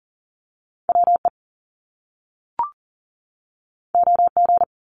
Morse code, heard as a keyed tone, about 38 wpm, 700 Hz. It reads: WE OG
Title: Top Two Letter Combinations in Words 40wpm